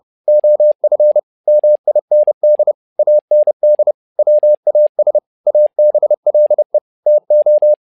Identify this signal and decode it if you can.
Morse code, heard as a keyed tone, about 30 wpm, 600 Hz. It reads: OF MIND AND WAS ABLE TO
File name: learning center / tower trouble / tower trouble 30wpm